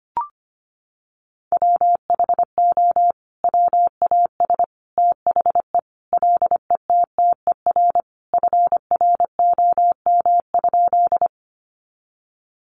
Morse code, heard as a keyed tone, about 25 wpm, 700 Hz. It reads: WHO WAS THE LETTER FROM?
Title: Sentences from Top 300 Words 25wpm